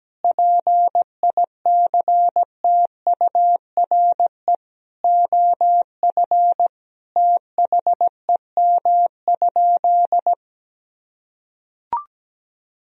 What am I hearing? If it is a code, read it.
Morse code, heard as a keyed tone, about 17 wpm, 700 Hz. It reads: PICTURE OF THEM?
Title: Sentences from Top 1000 Words - Review Part 2 - 17wpm